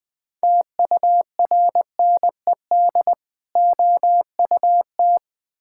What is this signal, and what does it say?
Morse code, heard as a keyed tone, about 20 wpm, 700 Hz. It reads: TURNED OUT
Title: Build A Phrase - 5-Word Phrases - 20wpm